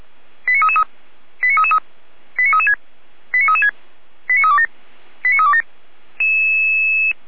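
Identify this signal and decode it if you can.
Five-tone selective call (ZVEI), heard as quick runs of five tones, tones of about 70 ms. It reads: 89333 89333 89336 89336 89327 89327
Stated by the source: Five-tone ZVEI selective call received by SDR.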